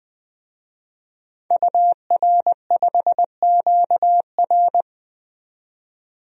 Morse code, heard as a keyed tone, about 20 wpm, 700 Hz. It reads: UR5QR
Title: DXCC-100 Call Signs - 5 Characters - 20wpm